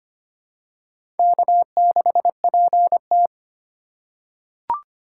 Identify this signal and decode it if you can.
Morse code, heard as a keyed tone, about 25 wpm, 700 Hz. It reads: K6PT
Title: USA Call Signs - 4 Characters - 3x Character Spacing - 25wpm